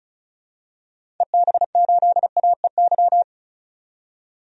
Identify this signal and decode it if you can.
Morse code, heard as a keyed tone, about 36 wpm, 700 Hz. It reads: EB8AEY
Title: DXCC-100 Call Signs - 6 Characters - 35wpm